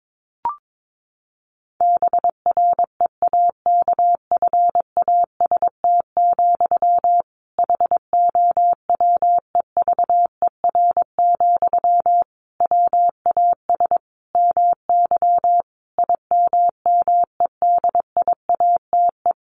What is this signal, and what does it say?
Morse code, heard as a keyed tone, about 22 wpm, 700 Hz. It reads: BREAKFAST, HOWEVER, WAS MY IMMEDIATE
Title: Sentences from Top 1500 Words 22wpm